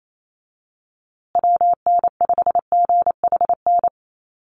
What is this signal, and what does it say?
Morse code, heard as a keyed tone, about 28 wpm, 700 Hz. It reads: WN5GHN
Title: USA Call Signs Repeated 2x 28wpm